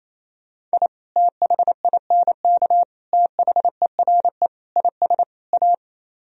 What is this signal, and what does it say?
Morse code, heard as a keyed tone, about 35 wpm, 700 Hz. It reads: I THINK THERE IS A
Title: Speed-Racing - Sets of 5 Words 28wpm